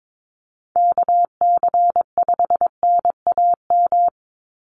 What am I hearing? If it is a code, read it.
Morse code, heard as a keyed tone, about 22 wpm, 700 Hz. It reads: KC5NAM
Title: USA Call Signs 22wpm